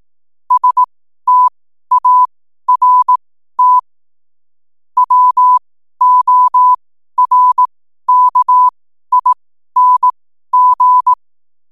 Morse code, heard as a keyed tone, about 18 wpm, spaced as for 12 wpm, 1000 Hz. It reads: START WORKING
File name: n7net / 12wpm